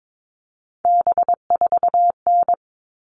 Morse code, heard as a keyed tone, about 22 wpm, 700 Hz. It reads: B4N
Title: All Cut Numbers - 3-Digit 22wpm